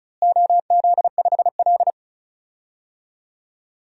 Morse code, heard as a keyed tone, about 34 wpm, 700 Hz. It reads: OZ5L
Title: DXCC-100 Call Signs - 4 Characters - Repeated 2x - 35wpm